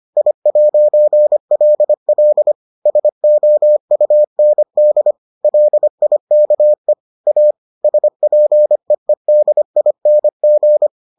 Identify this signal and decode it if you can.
Morse code, heard as a keyed tone, about 25 wpm, 600 Hz. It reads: I'LL SOUND LIKE A SPEEDING